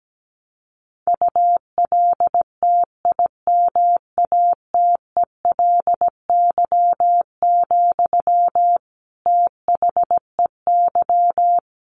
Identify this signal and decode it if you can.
Morse code, heard as a keyed tone, about 17 wpm, 700 Hz. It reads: ULTIMATELY, THEY
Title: Sentences from Top 1200 Words 17wpm